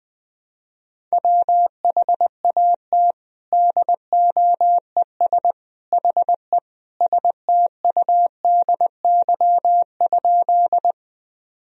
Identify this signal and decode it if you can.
Morse code, heard as a keyed tone, about 20 wpm, 700 Hz. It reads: WHAT DOES HE STUDY?
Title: Sentences from Top 1000 Words - Review Part 1 - 20wpm